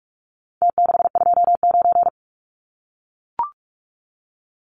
Morse code, heard as a keyed tone, about 44 wpm, 700 Hz. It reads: T629
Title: Numbers - Contest Style - 4-Digit 45wpm